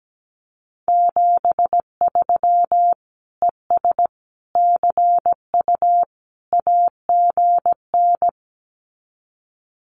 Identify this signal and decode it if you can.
Morse code, heard as a keyed tone, about 17 wpm, 700 Hz. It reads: 73 ES CU AGN
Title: Common QSO Phrases Encore 17wpm